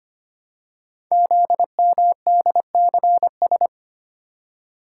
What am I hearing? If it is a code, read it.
Morse code, heard as a keyed tone, about 25 wpm, 700 Hz. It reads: ZMDCS
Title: Code Groups - 5 Letters - 25wpm